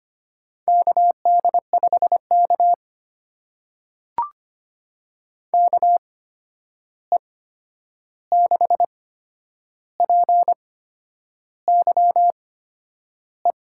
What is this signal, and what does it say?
Morse code, heard as a keyed tone, about 25 wpm, spaced as for 9 wpm, 700 Hz. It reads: KD5K KE6PYE
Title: USA Call Signs - 8x Character Spacing - 25wpm